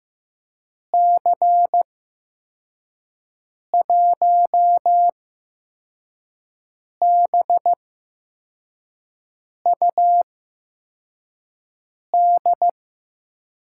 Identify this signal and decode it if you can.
Morse code, heard as a keyed tone, about 15 wpm, spaced as for 4 wpm, 700 Hz. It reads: C1BUD
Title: USA Call Signs - 6 Characters - 8x Character Spacing - 15wpm